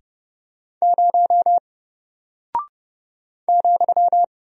Morse code, heard as a keyed tone, about 30 wpm, 700 Hz. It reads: 0 ,
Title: Single Character (Letter:Number:Punctuation:Prosign) Alt 30wpm